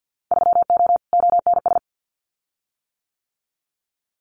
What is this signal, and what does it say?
Morse code, heard as a keyed tone, about 50 wpm, 700 Hz. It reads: 3O ONS
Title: Winter Field Day Exchanges Encore 50wpm